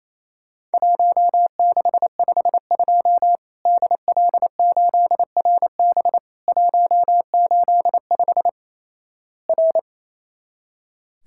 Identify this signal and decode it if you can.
Morse code, heard as a keyed tone, about 28 wpm, 700 Hz. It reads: 1652 DL8RB 185 R
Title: Worked All Europe DX Contest - QTC Exchange - 28wpm